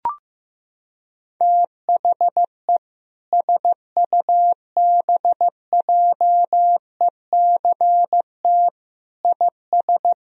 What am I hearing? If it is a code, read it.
Morse code, heard as a keyed tone, about 16 wpm, 700 Hz. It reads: THE SUBJECT IS
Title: Sentences from Top 1000 Words - Review Part 4 - 15wpm